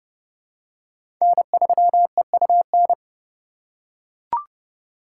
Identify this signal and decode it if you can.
Morse code, heard as a keyed tone, about 30 wpm, 700 Hz. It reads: N3EUN
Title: USA Call Signs - 2x Character Spacing - 30wpm